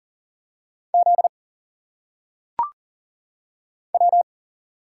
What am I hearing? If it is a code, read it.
Morse code, heard as a keyed tone, about 40 wpm, 700 Hz. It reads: Z W
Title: Single Letters - Phonetic Pronunciation 40wpm